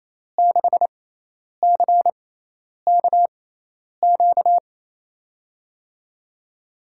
Morse code, heard as a keyed tone, about 28 wpm, spaced as for 10 wpm, 700 Hz. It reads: 6CKQ
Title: USA Call Signs - 6 Characters - 6x Character Spacing - 28wpm